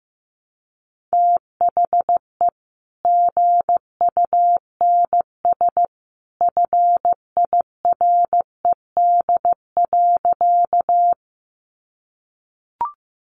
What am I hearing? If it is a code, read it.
Morse code, heard as a keyed tone, about 15 wpm, 700 Hz. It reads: THE GUNS FIRED.
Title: A Sound Of Thunder 15wpm